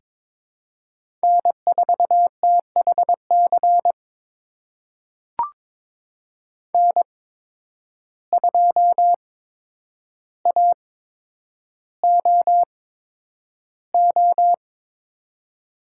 Morse code, heard as a keyed tone, about 22 wpm, spaced as for 9 wpm, 700 Hz. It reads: N4THC N2AOO
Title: USA Call Signs - 5 Characters - 8x Character Spacing - 22wpm